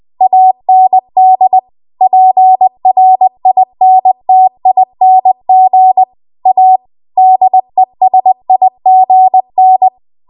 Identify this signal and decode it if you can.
Morse code, heard as a keyed tone, about 20 wpm, 750 Hz. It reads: AND PRINTING A DESIGN
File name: Archive / 20 WPM / 20WPM